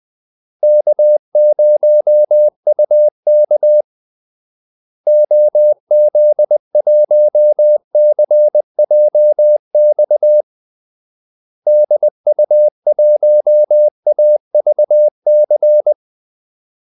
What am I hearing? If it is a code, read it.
Morse code, heard as a keyed tone, about 20 wpm, 600 Hz. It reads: K0UK OZ1CJX DU1AVC
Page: Kaldesignaler 20 wpm